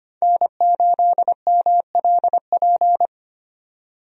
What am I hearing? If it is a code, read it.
Morse code, heard as a keyed tone, about 25 wpm, 700 Hz. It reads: N8MLP
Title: USA Call Signs - 5 Characters - 25wpm